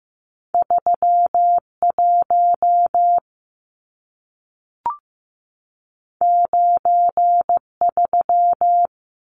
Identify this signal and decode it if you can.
Morse code, heard as a keyed tone, about 15 wpm, 700 Hz. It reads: 31 93
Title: Numbers - 2-Digit 15wpm